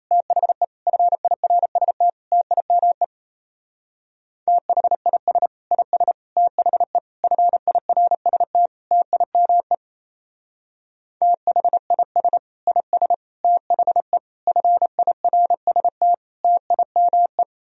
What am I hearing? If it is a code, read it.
Morse code, heard as a keyed tone, about 35 wpm, 700 Hz. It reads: THE FIRST TIME THIS IS THE FIRST TIME THIS IS THE FIRST TIME